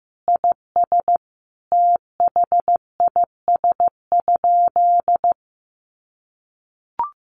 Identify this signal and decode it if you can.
Morse code, heard as a keyed tone, about 15 wpm, 700 Hz. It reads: IS THIS?
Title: Sentences from Top 1200 Words 15wpm